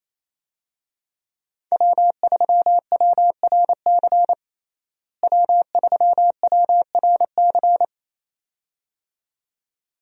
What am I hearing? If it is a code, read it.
Morse code, heard as a keyed tone, about 28 wpm, 700 Hz. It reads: W3WRC W3WRC
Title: USA Call Signs - 5 Characters - Repeated 2x - 28wpm